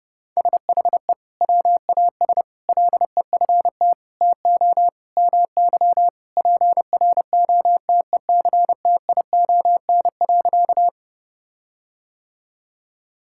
Morse code, heard as a keyed tone, about 30 wpm, 700 Hz. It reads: SHE WAS LEFT TO MY PROTECTION.
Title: Sentences from Top 1600 Words 30wpm